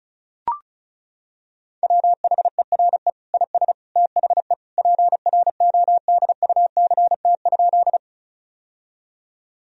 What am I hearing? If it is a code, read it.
Morse code, heard as a keyed tone, about 36 wpm, 700 Hz. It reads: WHERE IS THE PRODUCT?
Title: Sentences from Top 1000 Words - Review Part 2 - 35wpm